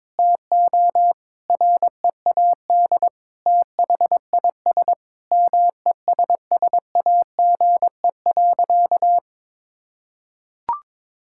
Morse code, heard as a keyed tone, about 22 wpm, 700 Hz. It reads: TO READ THIS MESSAGE.
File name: Sentences with 9-Word Phrases - 22wpm